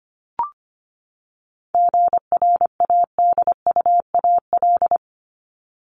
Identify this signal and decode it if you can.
Morse code, heard as a keyed tone, about 25 wpm, 700 Hz. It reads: GRADUAL
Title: LY Suffix - Top 100 Words with Base Words 25wpm